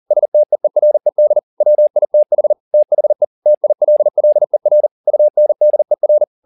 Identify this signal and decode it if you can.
Morse code, heard as a keyed tone, about 40 wpm, 600 Hz. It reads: STEERED WITH THE TILLER UNDER
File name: the old man and the sea 2 40wpm